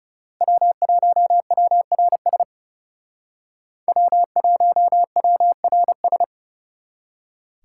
Morse code, heard as a keyed tone, about 34 wpm, 700 Hz. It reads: W1WRS W1WRS
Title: Speed Racing - USA 1x3 Call Signs 30wpm